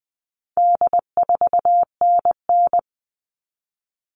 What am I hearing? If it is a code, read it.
Morse code, heard as a keyed tone, about 20 wpm, 700 Hz. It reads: D4NN
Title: All Cut Numbers - 4-Digit 20wpm